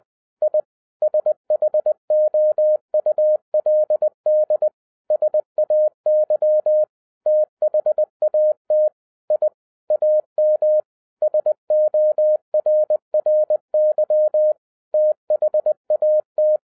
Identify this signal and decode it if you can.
Morse code, heard as a keyed tone, about 20 wpm, 610 Hz. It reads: I SHOULD SAY THAT I AM SORRY THAT
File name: 2012-01-mp3 / n16-THROUGH EIGHT MONTHS-20wpm